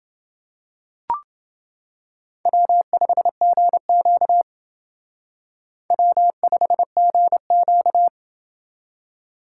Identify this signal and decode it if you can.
Morse code, heard as a keyed tone, about 28 wpm, 700 Hz. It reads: W5GQ W5GQ